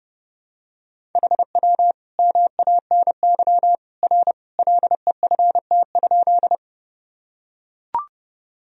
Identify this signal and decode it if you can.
Morse code, heard as a keyed tone, about 32 wpm, 700 Hz. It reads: HW MANY R LEFT?
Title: Sentences from Top 300 Words and Abbreviations 30wpm